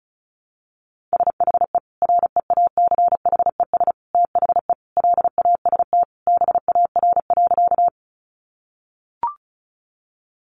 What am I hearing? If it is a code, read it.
Morse code, heard as a keyed tone, about 34 wpm, 700 Hz. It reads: SHE REACHES THE LAST BAR.
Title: Easy Phrases 35wpm